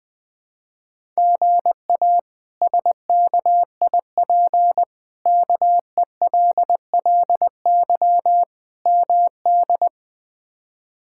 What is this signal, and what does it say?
Morse code, heard as a keyed tone, about 20 wpm, 700 Hz. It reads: GA SKIP KELLY MD
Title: K1USN SST QSO Practice - 20wpm